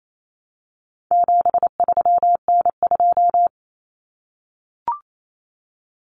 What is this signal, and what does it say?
Morse code, heard as a keyed tone, about 28 wpm, 700 Hz. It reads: 73N2
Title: Common Cut Numbers - 4-Digit 28wpm